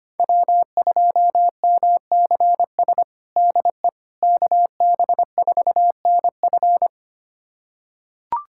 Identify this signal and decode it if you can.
Morse code, heard as a keyed tone, about 25 wpm, 700 Hz. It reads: W2MCS DE KB4NF
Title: USA Callsign DE Callsign - 5 Characters - 25wpm